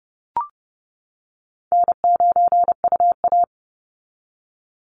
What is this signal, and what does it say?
Morse code, heard as a keyed tone, about 30 wpm, 700 Hz. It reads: N9UA
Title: Easy Call Signs - 30wpm